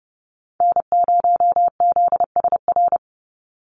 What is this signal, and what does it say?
Morse code, heard as a keyed tone, about 30 wpm, 700 Hz. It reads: N0ZSR
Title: USA Call Signs 30wpm